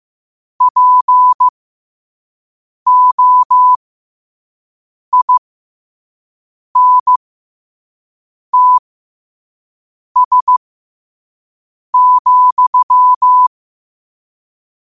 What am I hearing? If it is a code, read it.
Morse code, heard as a keyed tone, about 15 wpm, spaced as for 5 wpm, 1000 Hz. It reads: POINTS,
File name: qotdfiles / 5wpm